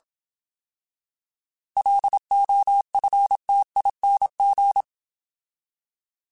Morse code, heard as a keyed tone, about 27 wpm, 800 Hz. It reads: LOFTING